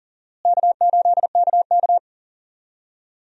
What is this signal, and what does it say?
Morse code, heard as a keyed tone, about 37 wpm, 700 Hz. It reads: K8KK
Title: USA Call Signs - 4 Characters - Repeated 2x - 40wpm